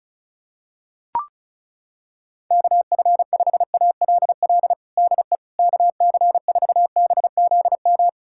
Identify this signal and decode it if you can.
Morse code, heard as a keyed tone, about 36 wpm, 700 Hz. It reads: KF5ALL DE KC4BZM
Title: USA Callsign DE Callsign - 6 Characters - 35wpm